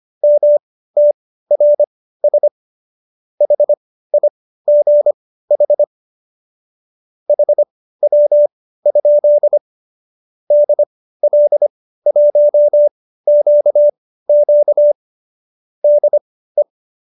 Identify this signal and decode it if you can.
Morse code, heard as a keyed tone, about 25 wpm, spaced as for 15 wpm, 590 Hz. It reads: MTRS HIGH HW? DL1QQ DE